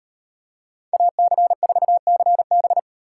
Morse code, heard as a keyed tone, about 38 wpm, 700 Hz. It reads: AC4CB